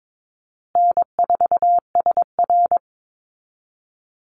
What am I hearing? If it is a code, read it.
Morse code, heard as a keyed tone, about 22 wpm, 700 Hz. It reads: N4SR
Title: Easy Call Signs Repeated 3x - 22wpm